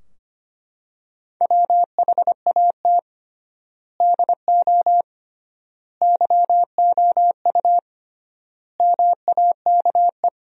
Morse code, heard as a keyed tone, about 25 wpm, 700 Hz. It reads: WHAT DO YOU MAKE
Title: 3x Word-Spacing - Sentences from Top 100 Words - 25wpm